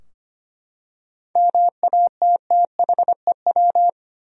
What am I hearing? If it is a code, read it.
Morse code, heard as a keyed tone, about 25 wpm, 700 Hz. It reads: MATTHEW
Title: USA Common Names - 25wpm